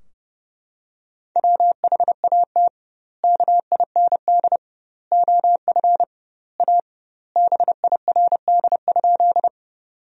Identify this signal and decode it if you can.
Morse code, heard as a keyed tone, about 30 wpm, 700 Hz. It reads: WHAT KIND OF A BIRD?
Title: 2x Word-Spacing - Sentences from Top 400 Words - 30wpm